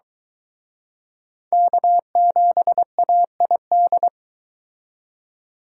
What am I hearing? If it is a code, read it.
Morse code, heard as a keyed tone, about 23 wpm, 700 Hz. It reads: K7AID